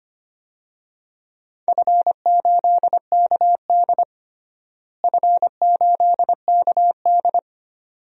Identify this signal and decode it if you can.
Morse code, heard as a keyed tone, about 25 wpm, 700 Hz. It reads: F8KD F8KD
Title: DXCC-100 Call Signs - 4 Characters - Repeated 2x - 25wpm